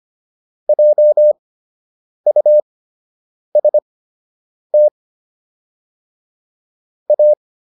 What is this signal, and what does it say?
Morse code, heard as a keyed tone, about 25 wpm, spaced as for 8 wpm, 600 Hz. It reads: JUST A
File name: WD104 08